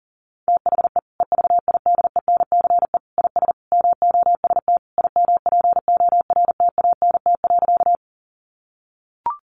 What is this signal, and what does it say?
Morse code, heard as a keyed tone, about 40 wpm, 700 Hz. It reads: THE EVIDENCE IS MOST IMPORTANT.
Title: Sentences from Top 1300 Words 40wpm